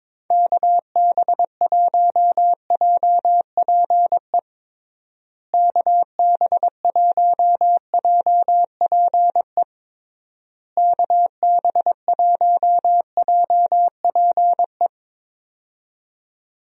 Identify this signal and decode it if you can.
Morse code, heard as a keyed tone, about 22 wpm, 700 Hz. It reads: KB1JPE KB1JPE KB1JPE
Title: USA Call Signs - 6 Characters - Repeated 3x - 22wpm